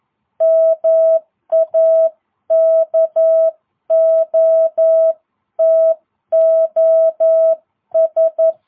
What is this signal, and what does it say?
Morse code, heard as a keyed tone, about 11 wpm, 660 Hz. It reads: MAKOTOS